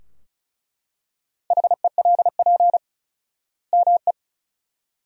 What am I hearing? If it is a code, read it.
Morse code, heard as a keyed tone, about 34 wpm, 700 Hz. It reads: HELP ME
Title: 4x Word-Spacing - Sentences from Top 500 Words - 35wpm